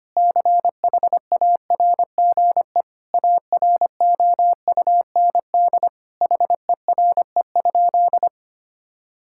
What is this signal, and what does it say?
Morse code, heard as a keyed tone, about 25 wpm, 700 Hz. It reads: CHARGE AROUND HERE?
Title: Sentences from Top 1000 Words - Review Part 6 - 25wpm